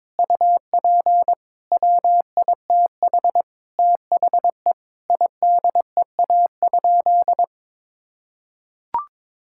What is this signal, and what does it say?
Morse code, heard as a keyed tone, about 22 wpm, 700 Hz. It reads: UP WITH THE IDEA?
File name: practice / 22wpm / Sentences with 8-Word Phrases - 22wpm